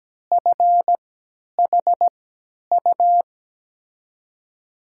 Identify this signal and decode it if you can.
Morse code, heard as a keyed tone, about 17 wpm, spaced as for 10 wpm, 700 Hz. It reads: FHU
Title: USA Call Signs - 6 Characters - 3x Character Spacing - 17wpm